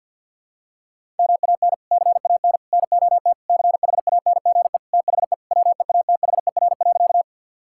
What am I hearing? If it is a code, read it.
Morse code, heard as a keyed tone, about 53 wpm, 700 Hz. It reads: MAN CAN NOT CHANGE THE WEATHER.
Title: Sentences from Top 1000 Words - Review Part 5 - 50wpm